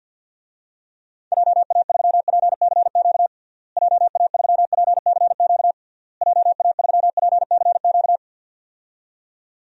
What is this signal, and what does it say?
Morse code, heard as a keyed tone, about 47 wpm, 700 Hz. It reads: JA3PCQ JA3PCQ JA3PCQ
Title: DXCC-100 Call Signs - 6 Characters - Repeated 3x - 50wpm